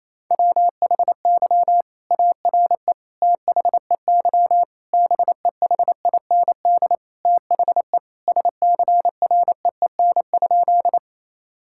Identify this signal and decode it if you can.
Morse code, heard as a keyed tone, about 28 wpm, 700 Hz. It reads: WHY ARE THEY BEHIND THE SCREEN?
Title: Sentences from Top 1000 Words - Review Part 6 - 28wpm